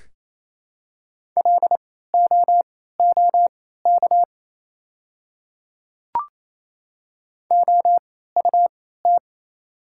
Morse code, heard as a keyed tone, about 28 wpm, spaced as for 16 wpm, 700 Hz. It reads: LOOK OUT
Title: 3x Character Spacing - Top 100 Words - 28wpm